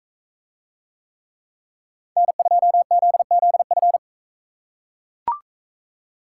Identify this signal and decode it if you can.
Morse code, heard as a keyed tone, about 41 wpm, 700 Hz. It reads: N2ZZF